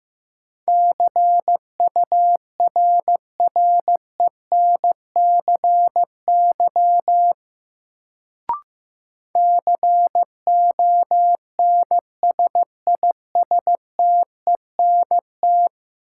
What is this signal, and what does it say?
Morse code, heard as a keyed tone, about 15 wpm, 700 Hz. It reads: CURRENCY CONSISTENT
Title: Sentences from Top 1500 Words - New Word Drill - 15wpm